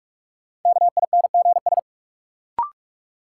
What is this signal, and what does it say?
Morse code, heard as a keyed tone, about 42 wpm, 700 Hz. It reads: KINGS
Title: Sentences from Top 1000 Words - Review Part 6 - New Word Drill - 45wpm